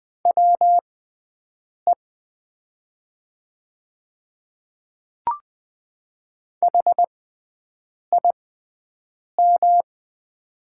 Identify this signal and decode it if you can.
Morse code, heard as a keyed tone, about 20 wpm, spaced as for 7 wpm, 700 Hz. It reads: WE HIM
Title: 6x Character Spacing - Top 100 Words - 20wpm